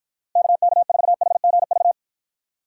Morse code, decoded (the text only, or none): KK4HGV